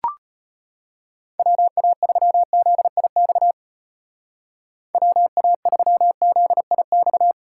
Morse code, heard as a keyed tone, about 31 wpm, 700 Hz. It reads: WA3ZIX WA3ZIX
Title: Speed Racing - USA Call Signs 25wpm